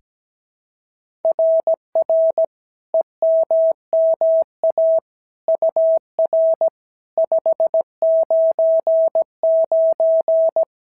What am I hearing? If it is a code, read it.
Morse code, heard as a keyed tone, about 16 wpm, 650 Hz. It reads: RR EMMA UR 599
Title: SOTA QSO Practice 17wpm